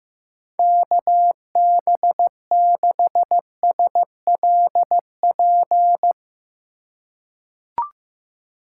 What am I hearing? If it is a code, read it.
Morse code, heard as a keyed tone, about 15 wpm, 700 Hz. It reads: KB6SLP
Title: USA Call Signs - 6 Characters - 6x Character Spacing - 15wpm